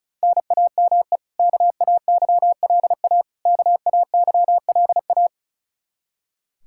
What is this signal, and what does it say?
Morse code, heard as a keyed tone, about 34 wpm, 700 Hz. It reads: NAME KAYLA KAYLA
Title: Practice QSO - Exchange 1 - Beginning - 35wpm